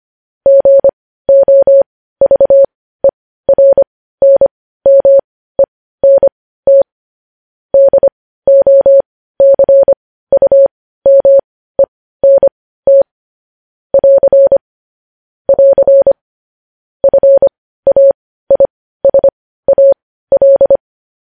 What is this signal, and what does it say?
Morse code, heard as a keyed tone, about 25 wpm, spaced as for 15 wpm, 550 Hz. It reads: GOVERNMENT DOCUMENT <AR> <AR> FAISAL
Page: news-headlines-in-morse-code-at-15-wpm